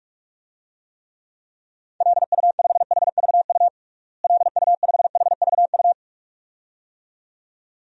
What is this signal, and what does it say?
Morse code, heard as a keyed tone, about 46 wpm, 700 Hz. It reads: LU5HVU LU5HVU
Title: DXCC-100 Call Signs - 6 Characters - Repeated 2x - 45wpm